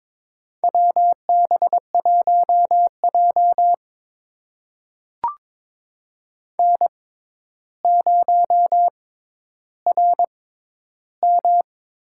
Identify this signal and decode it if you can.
Morse code, heard as a keyed tone, about 22 wpm, spaced as for 11 wpm, 700 Hz. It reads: WB1J N0RM
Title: USA Call Signs - 4 Characters - 6x Character Spacing - 22wpm